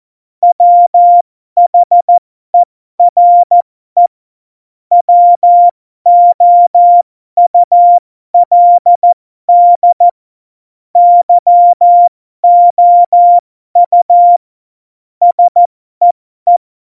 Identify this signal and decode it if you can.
Morse code, heard as a keyed tone, about 14 wpm, spaced as for 12 wpm, 700 Hz. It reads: WHERE WOULD YOU SEE
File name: Morsekurs / Witze